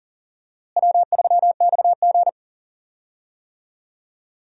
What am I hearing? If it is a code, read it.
Morse code, heard as a keyed tone, about 39 wpm, 700 Hz. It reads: W3XG
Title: USA Call Signs - 4 Characters - Repeated 2x - 40wpm